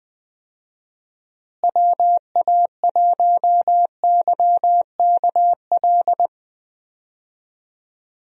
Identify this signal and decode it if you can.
Morse code, heard as a keyed tone, about 21 wpm, 700 Hz. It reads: WA1YKL
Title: USA Call Signs - 6 Characters - Repeated 2x - 20wpm